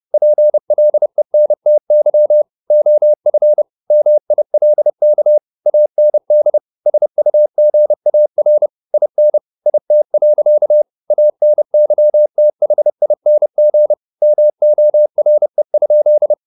Morse code, heard as a keyed tone, about 30 wpm, 600 Hz. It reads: PLENTY OF MILK AND SUGAR IN IT. ANYTHING MORE?